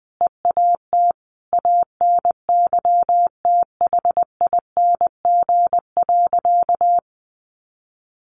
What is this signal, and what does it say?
Morse code, heard as a keyed tone, about 20 wpm, 700 Hz. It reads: EAT ANYTHING.
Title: Sentences from Top 1000 Words 20wpm